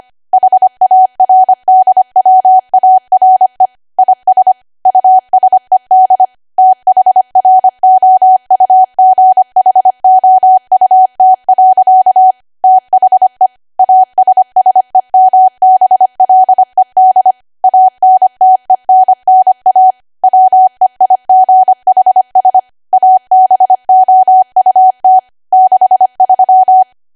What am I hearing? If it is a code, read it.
Morse code, heard as a keyed tone, about 25 wpm, 750 Hz. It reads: HARDWARE IS USED THROUGHOUT. THE ASSEMBLED ANTENNA WEIGHS ABOUT 63
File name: Archive / 25 WPM / 25WPM